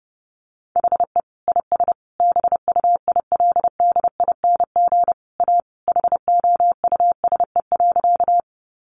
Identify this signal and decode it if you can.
Morse code, heard as a keyed tone, about 30 wpm, 700 Hz. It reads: HE IS BUILDING A HOUSE.
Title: Sentences from Top 700 Words 30wpm